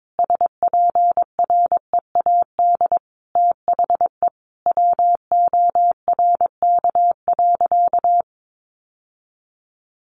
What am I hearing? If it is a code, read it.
Morse code, heard as a keyed tone, about 22 wpm, 700 Hz. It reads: SPREAD THE WORK.
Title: Sentences from Top 1400 Words 22wpm